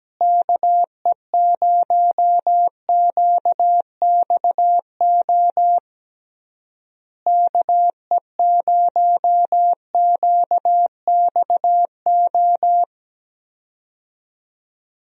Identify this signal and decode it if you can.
Morse code, heard as a keyed tone, about 17 wpm, 700 Hz. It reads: KE0QXO KE0QXO
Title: USA Call Signs - 6 Characters - Repeated 3x - 17wpm